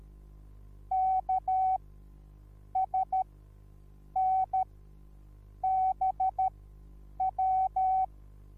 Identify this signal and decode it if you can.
Morse code, heard as a keyed tone, about 13 wpm, spaced as for 7 wpm, 760 Hz. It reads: KSNBW